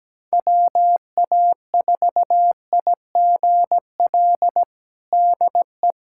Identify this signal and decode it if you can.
Morse code, heard as a keyed tone, about 17 wpm, 700 Hz. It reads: WA4IGL DE
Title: USA Callsign DE Callsign - 6 Characters - 17wpm